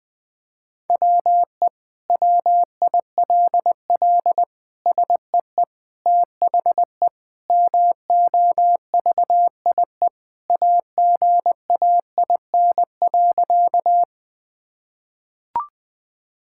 Morse code, heard as a keyed tone, about 21 wpm, 700 Hz. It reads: WE WILL SEE THE MOVIE AGAIN.
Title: Sentences from Top 1000 Words - Review Part 5 - 20wpm